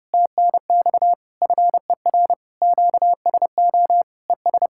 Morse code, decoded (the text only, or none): TNX FER QSO ES